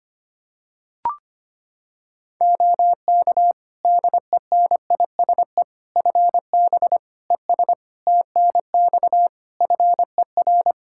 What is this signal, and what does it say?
Morse code, heard as a keyed tone, about 25 wpm, 700 Hz. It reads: OK DENISE FB ES TNX FER
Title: K1USN SST QSO Practice 25wpm